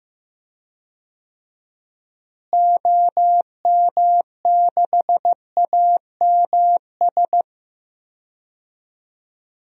Morse code, heard as a keyed tone, about 15 wpm, 700 Hz. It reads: OM6AMS
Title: DXCC-100 Call Signs - 6 Characters - Repeated 2x - 15wpm